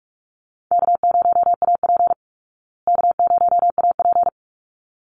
Morse code, heard as a keyed tone, about 44 wpm, 700 Hz. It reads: K0AP K0AP
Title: Easy Call Signs Repeated 2x - 45wpm